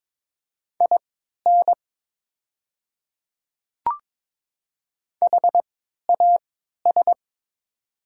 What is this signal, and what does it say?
Morse code, heard as a keyed tone, about 22 wpm, spaced as for 12 wpm, 700 Hz. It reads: IN HAS
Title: 3x Character Spacing - Top 100 Words - 22wpm